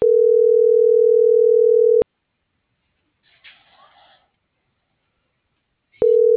Ringback tone (North America), 440 + 480 Hz.